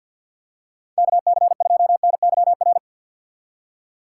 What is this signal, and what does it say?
Morse code, heard as a keyed tone, about 50 wpm, 700 Hz. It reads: KC2NYR
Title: USA Call Signs - 6 Characters - 4x Character Spacing - 50wpm